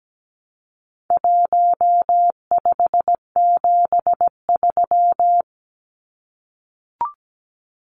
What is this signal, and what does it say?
Morse code, heard as a keyed tone, about 17 wpm, 700 Hz. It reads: 1573
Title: Numbers - Contest Style - 4-Digit 17wpm